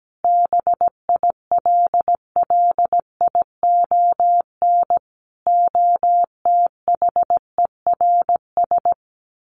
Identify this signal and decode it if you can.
Morse code, heard as a keyed tone, about 17 wpm, 700 Hz. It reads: BILLION OTHERS
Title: A Sound Of Thunder 17wpm